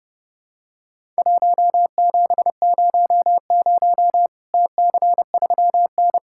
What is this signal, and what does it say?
Morse code, heard as a keyed tone, about 32 wpm, 700 Hz. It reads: 1700 TC3N
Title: Worked All Europe DX Contest - QTC Exchange - 30wpm